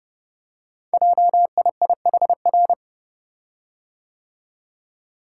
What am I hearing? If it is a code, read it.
Morse code, heard as a keyed tone, about 28 wpm, 700 Hz. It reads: JIIHR
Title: Code Groups - 5 Letters - 30wpm